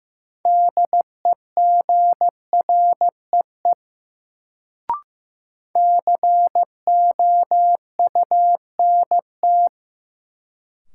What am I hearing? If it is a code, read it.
Morse code, heard as a keyed tone, about 15 wpm, 700 Hz. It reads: DEGREE COUNT
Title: Sentences from Top 1500 Words - New Word Drill - 15wpm